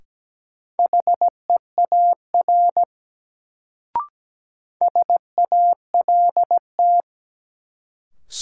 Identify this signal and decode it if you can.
Morse code, heard as a keyed tone, about 17 wpm, 700 Hz. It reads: HEAR SALT
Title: Sets of 5 Words - New Word Drill - 17wpm